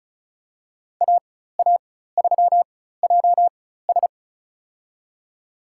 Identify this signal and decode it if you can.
Morse code, heard as a keyed tone, about 34 wpm, spaced as for 16 wpm, 700 Hz. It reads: AA3JS
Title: USA Call Signs - 5 Characters - 4x Character Spacing - 35wpm